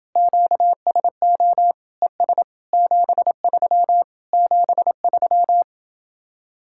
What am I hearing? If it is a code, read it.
Morse code, heard as a keyed tone, about 30 wpm, 700 Hz. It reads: QSO ES 73 73